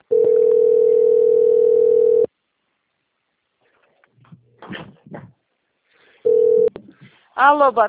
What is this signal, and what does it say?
Ringback tone (North America), 440 + 480 Hz.